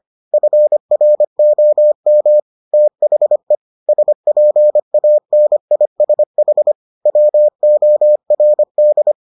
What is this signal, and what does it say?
Morse code, heard as a keyed tone, about 25 wpm, 600 Hz. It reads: FROM THE SPANISH WORD